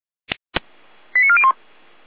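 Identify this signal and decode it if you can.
Five-tone selective call (ZVEI), heard as quick runs of five tones, tones of about 70 ms. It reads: 89401
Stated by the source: ZVEI five-tone selective call received on SDR.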